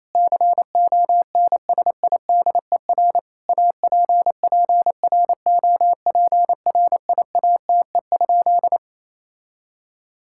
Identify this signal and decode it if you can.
Morse code, heard as a keyed tone, about 28 wpm, 700 Hz. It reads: CONSIDER APPROPRIATE?
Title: Sentences from Top 1000 Words - Review Part 5 - 28wpm